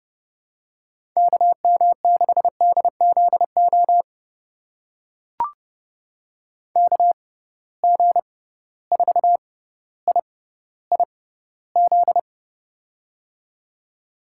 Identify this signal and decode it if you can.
Morse code, heard as a keyed tone, about 30 wpm, spaced as for 15 wpm, 700 Hz. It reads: KM6DZO KG4IIZ
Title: USA Call Signs - 6 Characters - 6x Character Spacing - 30wpm